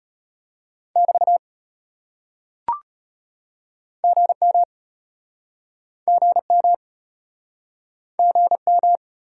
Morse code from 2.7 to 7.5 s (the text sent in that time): GM GM